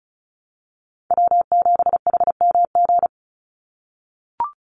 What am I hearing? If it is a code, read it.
Morse code, heard as a keyed tone, about 36 wpm, 700 Hz. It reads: W7HMG
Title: USA Call Signs 35wpm